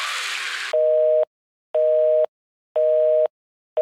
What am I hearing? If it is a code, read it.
Busy tone (North America), 480 + 620 Hz, with noise.